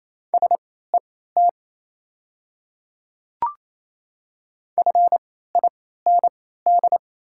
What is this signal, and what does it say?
Morse code, heard as a keyed tone, about 28 wpm, spaced as for 16 wpm, 700 Hz. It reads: SET FIND
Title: 3x Character Spacing - Top 200 Words - 28wpm